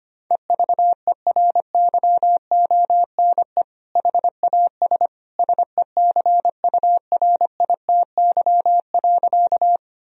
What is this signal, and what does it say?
Morse code, heard as a keyed tone, about 25 wpm, 700 Hz. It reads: EVERYONE HAS SECURITY.
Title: Sentences from Top 1000 Words - Review Part 4 - 25wpm